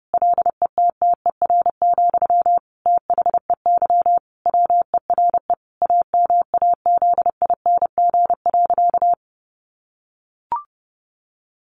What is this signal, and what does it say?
Morse code, heard as a keyed tone, about 30 wpm, 700 Hz. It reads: LETTER, THEY WERE AMAZING.
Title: Sentences from Top 1500 Words 30wpm